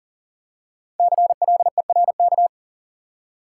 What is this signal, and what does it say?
Morse code, heard as a keyed tone, about 40 wpm, 700 Hz. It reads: CLERK
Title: Five Letter Words - 40wpm